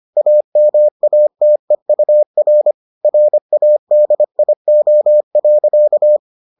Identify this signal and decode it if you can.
Morse code, heard as a keyed tone, about 25 wpm, 600 Hz. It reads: AMATEUR RADIO.